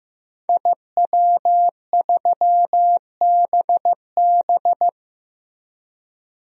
Morse code, heard as a keyed tone, about 15 wpm, 700 Hz. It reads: IW3BB
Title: DXCC-100 Call Signs - 5 Characters - Repeated 3x - 15wpm